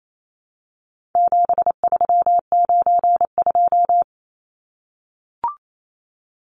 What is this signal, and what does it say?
Morse code, heard as a keyed tone, about 28 wpm, 700 Hz. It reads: 7392
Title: Numbers - 4-Digit 28wpm